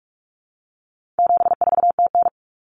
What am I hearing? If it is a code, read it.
Morse code, heard as a keyed tone, about 44 wpm, 700 Hz. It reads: 74TN
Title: Common Cut Numbers - 4-Digit 45wpm